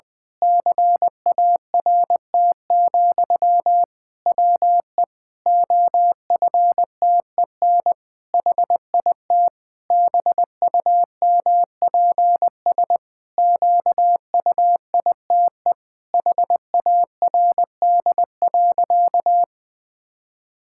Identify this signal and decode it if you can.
Morse code, heard as a keyed tone, about 20 wpm, 700 Hz. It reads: CART, WE OFTEN HIT BUMPS QUITE HARD.
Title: Pangram Challenge v2 - 20wpm